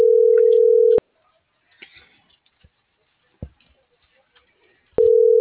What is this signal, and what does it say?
Ringback tone (North America), 440 + 480 Hz.